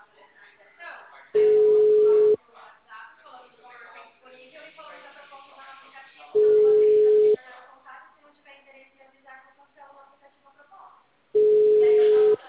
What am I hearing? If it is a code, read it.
Ringback tone (Europe, 425 Hz), with speech.
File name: out- -2001- - - .760